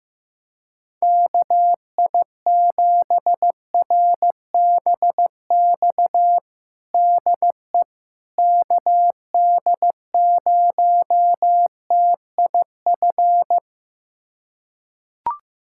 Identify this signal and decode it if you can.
Morse code, heard as a keyed tone, about 15 wpm, 700 Hz. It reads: KI7RBX DE KD0TIF
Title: USA Callsign DE Callsign - 6 Characters - 15wpm